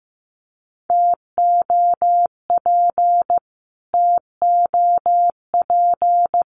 Morse code, heard as a keyed tone, about 15 wpm, 700 Hz. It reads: TOP TOP
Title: Top 500 Words Repeated 15wpm